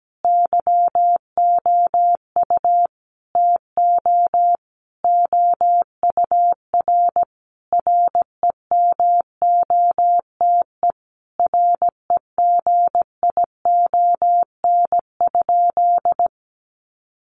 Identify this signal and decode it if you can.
Morse code, heard as a keyed tone, about 17 wpm, 700 Hz. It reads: YOU TO OUR REMOTE REGION?
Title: Sentences from Top 1500 Words 17wpm